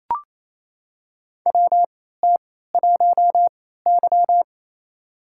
Morse code, heard as a keyed tone, about 29 wpm, spaced as for 16 wpm, 700 Hz. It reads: WT1Y
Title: USA Call Signs - 4 Characters - 3x Character Spacing - 28wpm